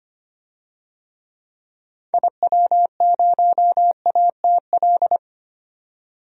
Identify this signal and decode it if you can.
Morse code, heard as a keyed tone, about 25 wpm, 700 Hz. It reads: IW0ATL
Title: DXCC-100 Call Signs - 6 Characters - 25wpm